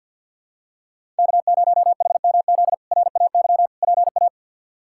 Morse code, heard as a keyed tone, about 52 wpm, 690 Hz. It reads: K0HMZ RAY PA
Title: K1USN SST QSO Practice - 50wpm